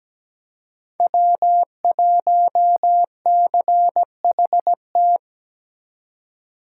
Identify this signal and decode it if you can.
Morse code, heard as a keyed tone, about 20 wpm, 700 Hz. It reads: W1CHT
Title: Speed Racing - USA Call Signs 17wpm